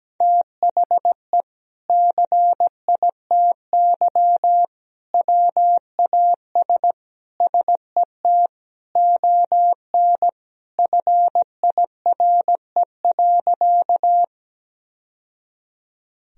Morse code, heard as a keyed tone, about 17 wpm, 700 Hz. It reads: THE CITY WAS SET ON FIRE.
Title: Sentences from Top 1000 Words - Review Part 1 - 17wpm